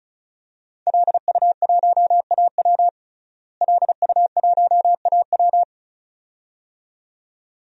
Morse code, heard as a keyed tone, about 36 wpm, 700 Hz. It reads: LU1AW LU1AW
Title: DXCC-100 Call Signs - 5 Characters - Repeated 2x - 35wpm